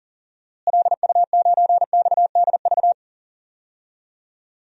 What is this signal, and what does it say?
Morse code, heard as a keyed tone, about 37 wpm, 700 Hz. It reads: LU9XDV
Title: DXCC-100 Call Signs - 6 Characters - Repeated 3x - 40wpm